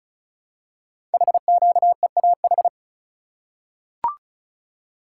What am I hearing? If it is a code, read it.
Morse code, heard as a keyed tone, about 36 wpm, 700 Hz. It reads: HQEAH
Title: Code Groups - 5 Letters - 35wpm